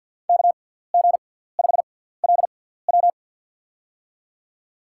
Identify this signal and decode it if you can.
Morse code, heard as a keyed tone, about 53 wpm, spaced as for 17 wpm, 690 Hz. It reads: KG5LW
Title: USA Call Signs - 6x Character Spacing - 50wpm